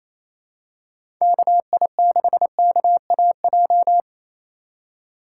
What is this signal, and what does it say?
Morse code, heard as a keyed tone, about 28 wpm, 700 Hz. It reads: KI6KAJ
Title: USA Call Signs - 6 Characters - 3x Character Spacing - 28wpm